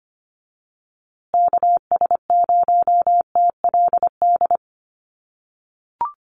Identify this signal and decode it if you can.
Morse code, heard as a keyed tone, about 25 wpm, 700 Hz. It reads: KS0TLD
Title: USA Call Signs Repeated 3x 25wpm